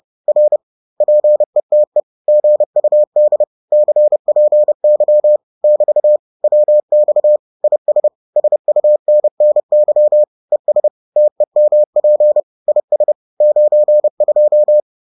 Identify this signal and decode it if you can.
Morse code, heard as a keyed tone, about 30 wpm, 600 Hz. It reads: R PETE GUD CPY <BT> WX IS SUNNY ES TEMP IS 92